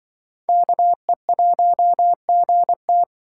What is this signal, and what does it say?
Morse code, heard as a keyed tone, about 25 wpm, 700 Hz. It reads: KE1GT
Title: Speed Racing - USA 2x2 Call Signs 20wpm